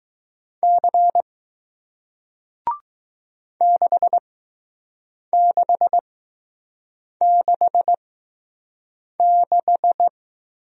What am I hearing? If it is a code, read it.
Morse code, heard as a keyed tone, about 21 wpm, spaced as for 6 wpm, 700 Hz. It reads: C 6666